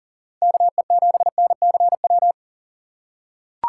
Morse code, heard as a keyed tone, about 40 wpm, 700 Hz. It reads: KE7NCW
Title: USA Call Signs - 6 Characters - 4x Character Spacing - 40wpm